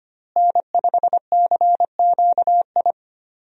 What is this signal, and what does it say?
Morse code, heard as a keyed tone, about 25 wpm, 700 Hz. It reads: N5CQI
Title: USA Call Signs - 5 Characters - Repeated 3x - 25wpm